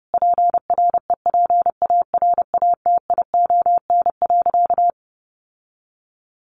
Morse code, heard as a keyed tone, about 30 wpm, 700 Hz. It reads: PREPARATION.
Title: Sentences from Top 1400 Words 30wpm